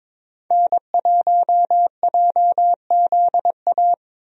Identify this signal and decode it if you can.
Morse code, heard as a keyed tone, about 21 wpm, 700 Hz. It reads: N1JZA
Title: USA Call Signs - 5 Characters - Repeated 2x - 22wpm